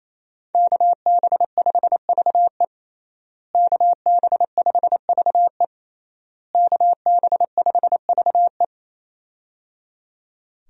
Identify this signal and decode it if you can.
Morse code, heard as a keyed tone, about 28 wpm, 700 Hz. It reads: KB5VE KB5VE KB5VE
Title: USA Call Signs - 5 Characters - Repeated 3x - 28wpm